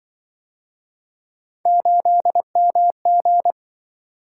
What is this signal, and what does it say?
Morse code, heard as a keyed tone, about 25 wpm, 700 Hz. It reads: 8MG